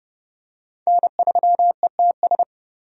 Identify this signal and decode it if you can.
Morse code, heard as a keyed tone, about 32 wpm, 700 Hz. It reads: N3ETS
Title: USA Call Signs - 5 Characters - 30wpm